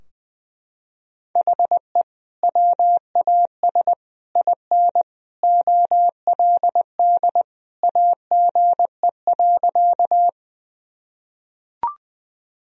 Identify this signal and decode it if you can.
Morse code, heard as a keyed tone, about 19 wpm, 700 Hz. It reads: HE WAS IN OLD AGE.
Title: Sentences from Top 1000 Words - Review Part 3 - 20wpm